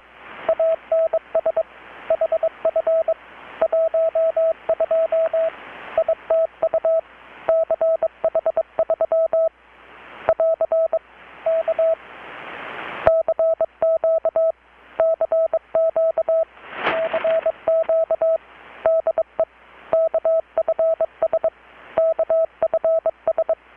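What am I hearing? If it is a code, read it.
Morse code, heard as a keyed tone, about 22 wpm, 650 Hz, with radio noise. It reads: ANS HF 12 ITU CH3 <AR> K CQ CQ CQ DE KFS KFS
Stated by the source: Morse from KPH's Night of Nights XX maritime radiotelegraphy event, received in Chiapas, Mexico about 3,800 km away on a Kenwood R-600 with a grounded 10 m long-wire antenna.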